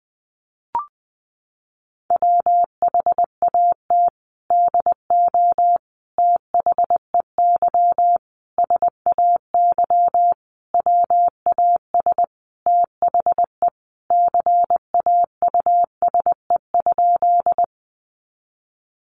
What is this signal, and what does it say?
Morse code, heard as a keyed tone, about 20 wpm, 700 Hz. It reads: WHAT DO THEY SAY WAS THE CAUSE?
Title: Sentences from Top 200 Words 20wpm